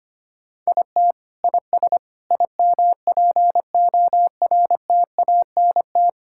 Morse code, heard as a keyed tone, about 25 wpm, 700 Hz. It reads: IT IS IMPORTANT
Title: Sentences from Top 1000 Words - Review Part 6 - 25wpm